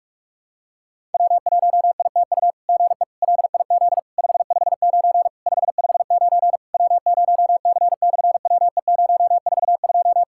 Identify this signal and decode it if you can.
Morse code, heard as a keyed tone, about 44 wpm, 700 Hz. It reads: W1ITU GE LIZ 559 559 W0C/WE042
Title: SOTA QSO Practice 45wpm